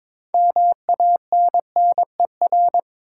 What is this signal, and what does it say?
Morse code, heard as a keyed tone, about 21 wpm, 700 Hz. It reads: MANNER
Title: Six Letter Words - 22wpm